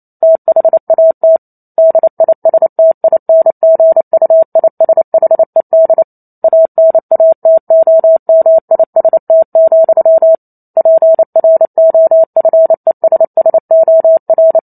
Morse code, heard as a keyed tone, about 28 wpm, 650 Hz. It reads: THAT DISTINGUISHED ANATOMIST, PROFESSOR
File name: Chapter 2 - 28 WPM